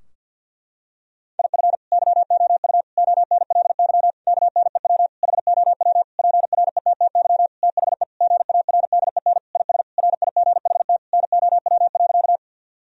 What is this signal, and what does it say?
Morse code, heard as a keyed tone, about 48 wpm, 700 Hz. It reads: IF YOU ONLY KNEW HOW PRETTY THE GARDEN IS RIGHT NOW.
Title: Sentences from Top 1000 Words - Review Part 4 - 50wpm